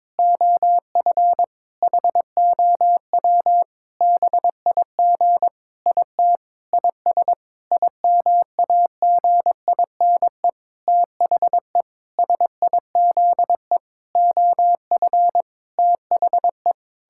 Morse code, heard as a keyed tone, about 22 wpm, 700 Hz. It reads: OF HOW BIG IT IS IMAGINE THE SIZE OF THE
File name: Sentences with 7-Word Phrases - 22wpm